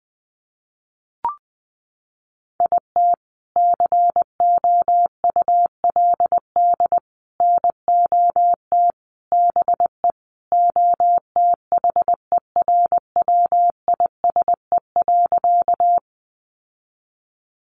Morse code, heard as a keyed tone, about 20 wpm, 700 Hz. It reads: IT COULD NOT BE OTHERWISE.
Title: Sentences from Top 1200 Words 20wpm